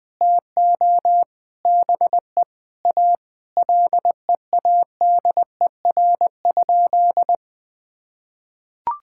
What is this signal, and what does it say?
Morse code, heard as a keyed tone, about 20 wpm, 700 Hz. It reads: TO BE A LEADER?
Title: Sentences with 7-Word Phrases - Spoken Beginning - 20wpm